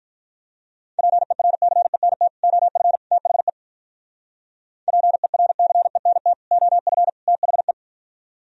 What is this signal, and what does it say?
Morse code, heard as a keyed tone, about 45 wpm, 700 Hz. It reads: PERCENT OF THE PERCENT OF THE